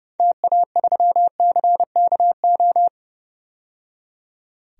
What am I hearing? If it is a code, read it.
Morse code, heard as a keyed tone, about 30 wpm, 700 Hz. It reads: TA3CKO
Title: DXCC-100 Call Signs - 6 Characters - 30wpm